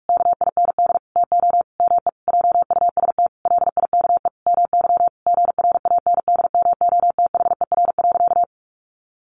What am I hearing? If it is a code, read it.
Morse code, heard as a keyed tone, about 44 wpm, 700 Hz. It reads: KIND TO ME JUST LIKE MY GRANDMOTHER.
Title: Sentences from Top 1200 Words 45wpm